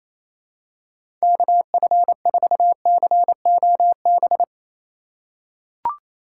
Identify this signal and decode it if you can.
Morse code, heard as a keyed tone, about 28 wpm, 700 Hz. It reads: KF4COB
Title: USA Call Signs - 6 Characters - 4x Character Spacing - 28wpm